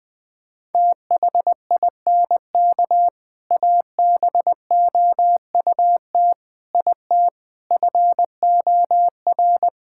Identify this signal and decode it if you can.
Morse code, heard as a keyed tone, about 21 wpm, 700 Hz. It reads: THINK ABOUT IT FOR
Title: Sentences with 3-Word Phrases - Spoken Beginning - 20wpm